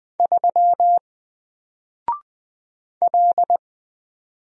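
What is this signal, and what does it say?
Morse code, heard as a keyed tone, about 20 wpm, 700 Hz. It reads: 3 L
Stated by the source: Single letter and number Morse drill at 20 wpm, from N4LCD.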